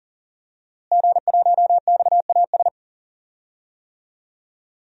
Morse code, heard as a keyed tone, about 40 wpm, 700 Hz. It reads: G1XAS
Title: DXCC-100 Call Signs - 5 Characters - Repeated 3x - 40wpm